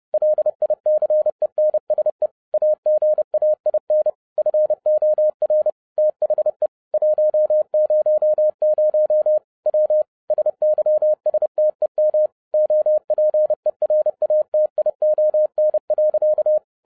Morse code, heard as a keyed tone, about 30 wpm, 610 Hz. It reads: LICENSE AGAIN FOR THE 100 W SYSTEM OPERATION.